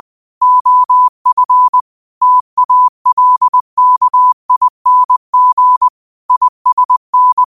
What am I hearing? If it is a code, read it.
Morse code, heard as a keyed tone, about 20 wpm, 1000 Hz. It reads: OF TALKING ISN